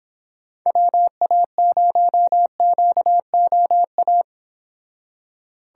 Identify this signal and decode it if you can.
Morse code, heard as a keyed tone, about 26 wpm, 700 Hz. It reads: WA0QOA